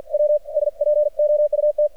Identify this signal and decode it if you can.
Morse code, heard as a keyed tone, about 53 wpm, 600 Hz. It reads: W S POUT
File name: archived / CRSnet